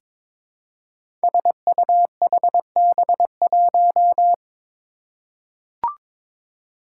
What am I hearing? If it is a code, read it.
Morse code, heard as a keyed tone, about 22 wpm, 700 Hz. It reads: SUHB1
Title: Five Challenging Letter-Number Combinations 22wpm